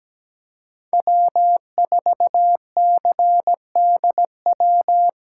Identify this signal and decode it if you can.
Morse code, heard as a keyed tone, about 17 wpm, 700 Hz. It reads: W4CDW
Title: USA Call Signs - 8x Character Spacing - 17wpm